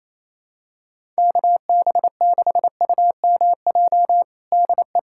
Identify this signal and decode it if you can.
Morse code, heard as a keyed tone, about 28 wpm, 700 Hz. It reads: KB6UMJ DE
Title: USA Callsign DE Callsign - 6 Characters - 28wpm